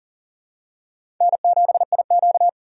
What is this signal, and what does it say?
Morse code, heard as a keyed tone, about 40 wpm, 700 Hz. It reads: N7IQ
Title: USA Call Signs - 4 Characters - 8x Character Spacing - 40wpm